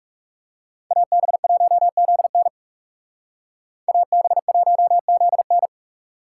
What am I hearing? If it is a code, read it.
Morse code, heard as a keyed tone, about 36 wpm, 700 Hz. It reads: AB1ZN AB1ZN
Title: Speed Racing - USA 2x2 Call Signs 30wpm